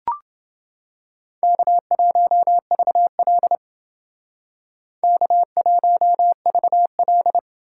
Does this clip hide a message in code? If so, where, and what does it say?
Yes: Morse code from 1.4 to 7.4 s, heard as a keyed tone, about 28 wpm, 700 Hz. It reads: K1VL K1VL